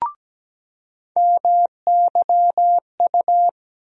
Morse code, heard as a keyed tone, about 17 wpm, 700 Hz. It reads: MYU
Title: USA Call Sign Postfixes - 3 Letters 17wpm